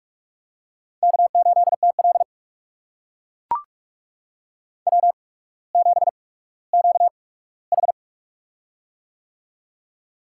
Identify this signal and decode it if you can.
Morse code, heard as a keyed tone, about 44 wpm, spaced as for 19 wpm, 700 Hz. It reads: K8TL W7QH
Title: USA Call Signs - 4 Characters - 8x Character Spacing - 45wpm